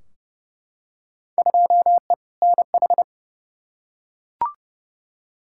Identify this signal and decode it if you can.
Morse code, heard as a keyed tone, about 30 wpm, 700 Hz. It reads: 2E NH
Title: Field Day Exchanges - 30wpm